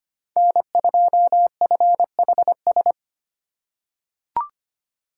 Morse code, heard as a keyed tone, about 25 wpm, 700 Hz. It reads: N2FHS
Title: USA Call Signs - 5 Characters - 6x Character Spacing - 25wpm